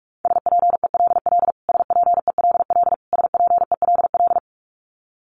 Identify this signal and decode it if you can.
Morse code, heard as a keyed tone, about 44 wpm, 700 Hz. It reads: SPELL SPELL SPELL
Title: Top 200 Words Repeated 45wpm